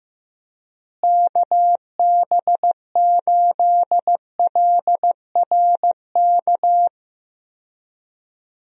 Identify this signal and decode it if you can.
Morse code, heard as a keyed tone, about 15 wpm, 700 Hz. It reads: KB8LRK
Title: USA Call Signs - 6 Characters - Repeated 3x - 15wpm